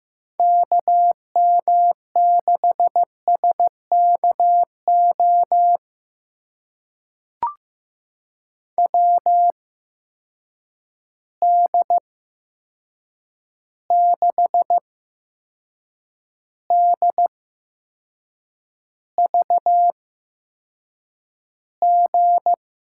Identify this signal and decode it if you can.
Morse code, heard as a keyed tone, about 15 wpm, spaced as for 6 wpm, 700 Hz. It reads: KM6SKO WD6DVG
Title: USA Call Signs - 6 Characters - 8x Character Spacing - 15wpm